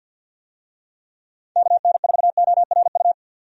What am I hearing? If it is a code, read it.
Morse code, heard as a keyed tone, about 48 wpm, 700 Hz. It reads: KN4ORU
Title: USA Call Signs - 6 Characters - 50wpm